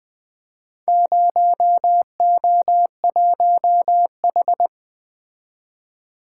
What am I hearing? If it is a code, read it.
Morse code, heard as a keyed tone, about 24 wpm, 700 Hz. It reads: 0O1H
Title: Speed-Racing - Four Letter-Number Combinations - 20wpm